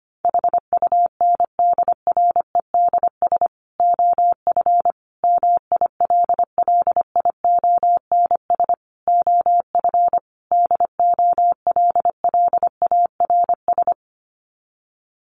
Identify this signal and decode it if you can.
Morse code, heard as a keyed tone, about 25 wpm, 700 Hz. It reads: HUNDREDS OF MILLIONS OF DOLLARS
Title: Sets of 5 Words 25wpm